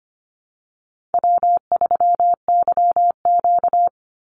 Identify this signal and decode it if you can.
Morse code, heard as a keyed tone, about 25 wpm, 700 Hz. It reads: W3YQ
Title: Call Signs - 25wpm